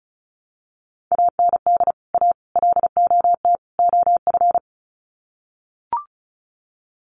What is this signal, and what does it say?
Morse code, heard as a keyed tone, about 34 wpm, 700 Hz. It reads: AND A LOT OF
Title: Sets of 4 Words 35wpm